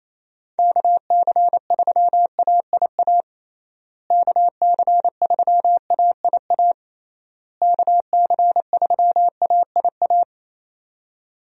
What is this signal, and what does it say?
Morse code, heard as a keyed tone, about 27 wpm, 700 Hz. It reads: KC3AIA KC3AIA KC3AIA
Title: USA Call Signs - 6 Characters - Repeated 3x - 28wpm